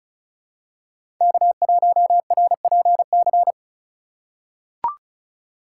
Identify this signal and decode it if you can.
Morse code, heard as a keyed tone, about 36 wpm, 700 Hz. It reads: K1RPC
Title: USA Call Signs - 5 Characters - 2x Character Spacing - 35wpm